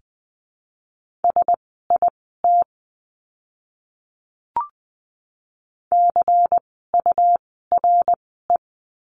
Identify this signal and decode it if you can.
Morse code, heard as a keyed tone, about 20 wpm, spaced as for 14 wpm, 700 Hz. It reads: SIT CURE
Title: 101 Lesson 17f - Letter u - Words 20wpm